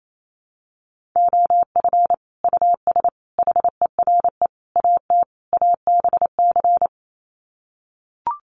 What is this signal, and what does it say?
Morse code, heard as a keyed tone, about 28 wpm, 700 Hz. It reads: OF US HERE AT ABC
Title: Sets of 5 Words 28wpm